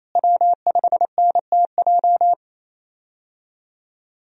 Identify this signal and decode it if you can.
Morse code, heard as a keyed tone, about 28 wpm, 700 Hz. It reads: W5NTJ
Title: USA Call Signs - 5 Characters - Repeated 3x - 28wpm